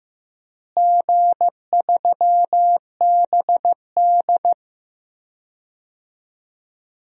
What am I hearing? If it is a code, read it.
Morse code, heard as a keyed tone, about 15 wpm, 700 Hz. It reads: G3BD
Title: DXCC-100 Call Signs - 4 Characters - Repeated 2x - 15wpm